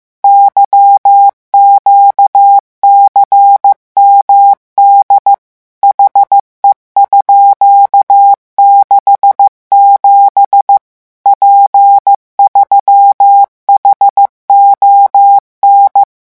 Morse code, heard as a keyed tone, about 15 wpm, 800 Hz. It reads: YQCMD HE_67 P3HON
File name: morse mp3 / revision13-vitesse 14 mots